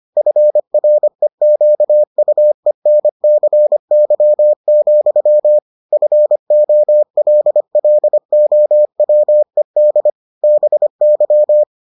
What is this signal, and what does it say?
Morse code, heard as a keyed tone, about 25 wpm, 600 Hz. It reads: FREQUENCY, FOLLOWED BY